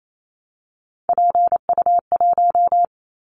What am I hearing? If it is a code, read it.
Morse code, heard as a keyed tone, about 28 wpm, 700 Hz. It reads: PU1
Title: Callsign Prefixes for Top 50 DXCC - 28wpm